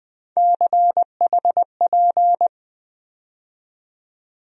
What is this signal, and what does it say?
Morse code, heard as a keyed tone, about 20 wpm, 700 Hz. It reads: CHP
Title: USA Call Sign Postfixes - 3 Letters 20wpm